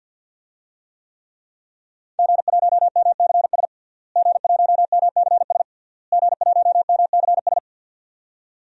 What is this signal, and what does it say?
Morse code, heard as a keyed tone, about 50 wpm, 700 Hz. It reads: G1MCS G1MCS G1MCS
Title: DXCC-100 Call Signs - 5 Characters - Repeated 3x - 50wpm